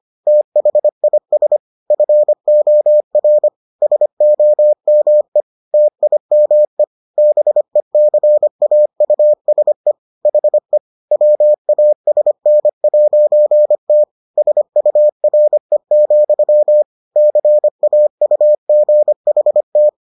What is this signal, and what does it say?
Morse code, heard as a keyed tone, about 25 wpm, 600 Hz. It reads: THIS FOR SOME TIME BECAUSE HE WASN'T SURE, CAUGHT